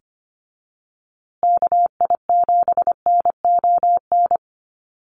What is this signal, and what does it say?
Morse code, heard as a keyed tone, about 25 wpm, 700 Hz. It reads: KI7NON
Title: USA Call Signs Repeated 3x 25wpm